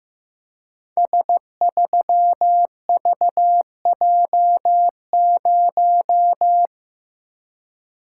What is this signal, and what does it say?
Morse code, heard as a keyed tone, about 15 wpm, 700 Hz. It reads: S3VJ0
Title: Code Groups - 5 Letter:Numbers - 15wpm